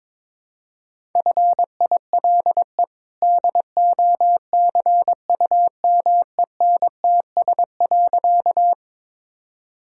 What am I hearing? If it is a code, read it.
Morse code, heard as a keyed tone, about 22 wpm, 700 Hz. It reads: FILE DOCUMENTS.
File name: practice / 22wpm / Easy Sentences with Hints - 22wpm